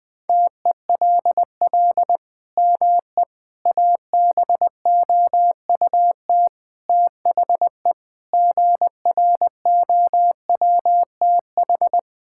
Morse code, heard as a keyed tone, about 20 wpm, 700 Hz. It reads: TELL ME ABOUT THE GROWTH
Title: Sentences from Top 1000 Words - Review Part 5 - 20wpm